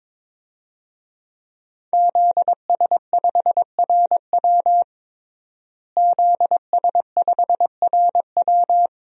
Morse code, heard as a keyed tone, about 22 wpm, 700 Hz. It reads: ZS5RW ZS5RW
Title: DXCC-100 Call Signs - 5 Characters - Repeated 3x - 22wpm